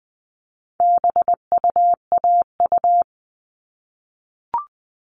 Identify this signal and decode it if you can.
Morse code, heard as a keyed tone, about 20 wpm, 700 Hz. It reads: BUAU
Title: All Cut Numbers - 4-Digit 20wpm